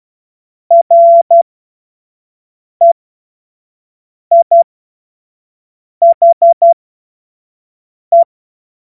Morse code, heard as a keyed tone, about 12 wpm, spaced as for 5 wpm, 680 Hz. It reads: REIHE
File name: Morsekurs / Witze / Witz 07. Juni 2022 25 F70